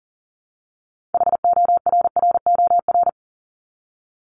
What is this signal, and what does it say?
Morse code, heard as a keyed tone, about 40 wpm, 700 Hz. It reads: HORROR
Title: Top 1501-1750 Words 40wpm